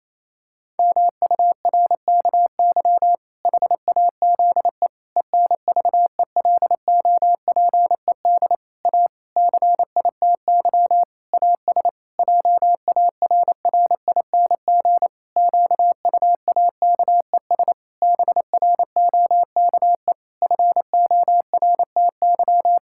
Morse code, heard as a keyed tone, about 28 wpm, 700 Hz. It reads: MURKY HAZE ENVELOPED A CITY AS JARRING QUAKES BROKE FORTY
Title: Pangram Challenge v2 - 28wpm